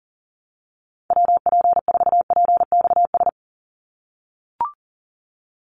Morse code, heard as a keyed tone, about 40 wpm, 700 Hz. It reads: WP4PXS
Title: USA Call Signs 40wpm